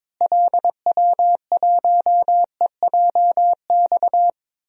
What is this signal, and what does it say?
Morse code, heard as a keyed tone, about 22 wpm, 700 Hz. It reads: LW1EJX
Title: DXCC-100 Call Signs - 6 Characters - Repeated 2x - 22wpm